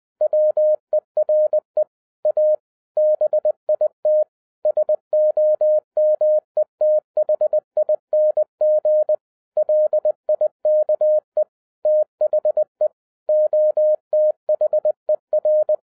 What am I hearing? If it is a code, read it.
Morse code, heard as a keyed tone, about 20 wpm, 610 Hz. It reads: WERE A BIT SOMETHING LIKE THE OTHER
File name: 2011-07-mp3 / n58 BACK FROM A BIG ACCIDENT 20wpm